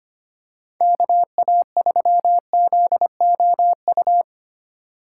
Morse code, heard as a keyed tone, about 25 wpm, 700 Hz. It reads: KA3ZOU
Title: USA Call Signs - 6 Characters - 4x Character Spacing - 25wpm